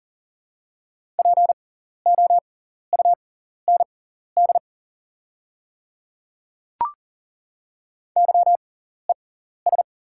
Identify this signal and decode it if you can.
Morse code, heard as a keyed tone, about 40 wpm, spaced as for 14 wpm, 700 Hz. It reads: POUND YES
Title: 6x Character Spacing - Top 500 Words - 40wpm